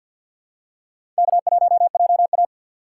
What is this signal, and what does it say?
Morse code, heard as a keyed tone, about 53 wpm, 700 Hz. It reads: K1JA
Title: USA Call Signs - 4 Characters - 3x Character Spacing - 50wpm